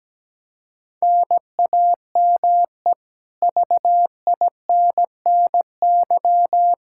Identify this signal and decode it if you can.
Morse code, heard as a keyed tone, about 17 wpm, 700 Hz. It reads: NAME VINNY
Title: Practice QSO - Exchange 1 - Beginning - 17wpm